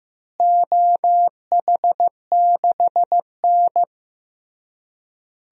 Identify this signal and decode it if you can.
Morse code, heard as a keyed tone, about 16 wpm, 700 Hz. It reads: OH6N
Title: DXCC-100 Call Signs - 4 Characters - Repeated 3x - 15wpm